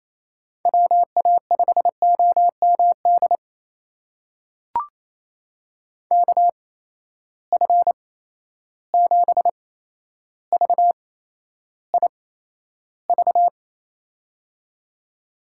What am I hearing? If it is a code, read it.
Morse code, heard as a keyed tone, about 28 wpm, spaced as for 12 wpm, 700 Hz. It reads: WA5OMD KF7VIV
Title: USA Call Signs - 6 Characters - 8x Character Spacing - 28wpm